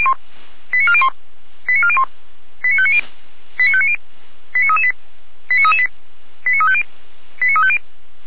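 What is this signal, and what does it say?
Five-tone selective call (ZVEI), heard as quick runs of five tones, tones of about 70 ms. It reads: 89401 89401 89590 89590 89338 89338 89350 89350
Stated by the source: ZVEI five-tone selective call received by SDR.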